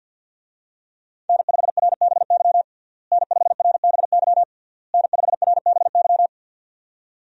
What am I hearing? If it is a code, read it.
Morse code, heard as a keyed tone, about 53 wpm, 700 Hz. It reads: N5RBY N5RBY N5RBY
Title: USA Call Signs - 5 Characters - Repeated 3x - 50wpm